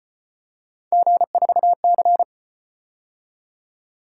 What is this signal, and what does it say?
Morse code, heard as a keyed tone, about 31 wpm, 700 Hz. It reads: G4C